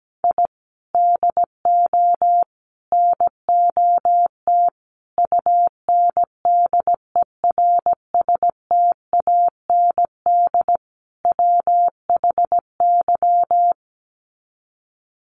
Morse code, heard as a keyed tone, about 17 wpm, 700 Hz. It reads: I DO NOT UNDERSTAND WHY
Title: Sets of 5 Words Encore 17wpm